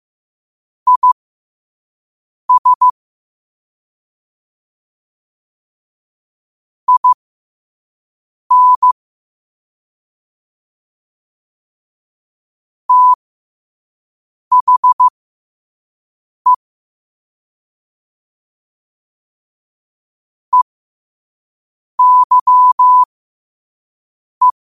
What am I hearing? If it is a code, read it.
Morse code, heard as a keyed tone, about 15 wpm, spaced as for 5 wpm, 1000 Hz. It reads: IS IN THE EYE
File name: qotdfiles / 5wpm